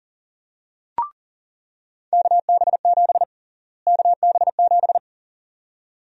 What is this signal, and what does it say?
Morse code, heard as a keyed tone, about 38 wpm, 700 Hz. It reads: KB7 KB7
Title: USA Call Sign Prefixes - Repeated 2x - 40wpm